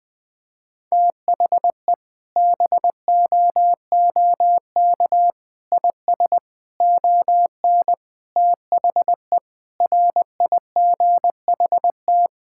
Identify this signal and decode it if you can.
Morse code, heard as a keyed tone, about 20 wpm, 700 Hz. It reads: THE BOOK IS ON THE RIGHT
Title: Sentences with 7-Word Phrases - 20wpm